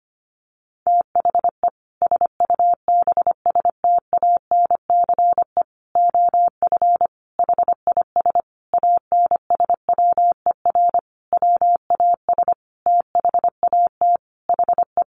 Morse code, heard as a keyed tone, about 25 wpm, 700 Hz. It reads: THE SUBSTANCE OF HIS ANSWER WAS THAT HE
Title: Sentences from Top 1500 Words 25wpm